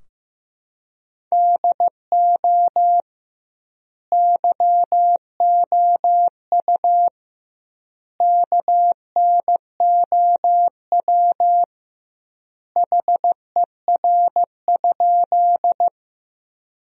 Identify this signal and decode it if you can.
Morse code, heard as a keyed tone, about 14 wpm, 700 Hz. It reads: DO YOU KNOW HER?
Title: 2x Word-Spacing - Sentences from Top 100 Words - 15wpm